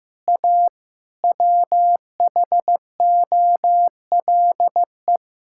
Morse code, heard as a keyed tone, about 15 wpm, 700 Hz. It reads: A WHOLE
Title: Sentences from Top 1000 Words - Review Part 6 - 15wpm